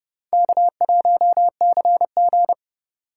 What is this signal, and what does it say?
Morse code, heard as a keyed tone, about 29 wpm, 700 Hz. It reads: K1CG